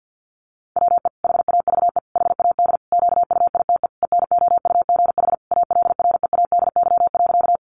Morse code, heard as a keyed tone, about 50 wpm, 700 Hz. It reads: WE HAVE HAD QUITE ENOUGH ALREADY.
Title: Sentences from Top 700 Words 50wpm